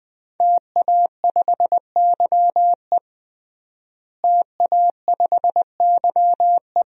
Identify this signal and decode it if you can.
Morse code, heard as a keyed tone, about 20 wpm, 700 Hz. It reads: TA5YE TA5YE
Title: DXCC-100 Call Signs - 5 Characters - Repeated 3x - 20wpm